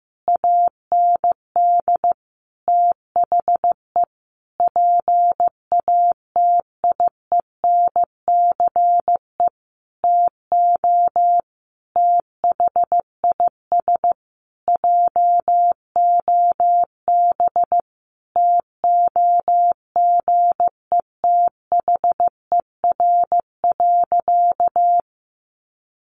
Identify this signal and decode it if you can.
Morse code, heard as a keyed tone, about 15 wpm, 700 Hz. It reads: AND THE PATIENCE TO THIS JOB TOGETHER.
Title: Sentences from Top 1300 Words 15wpm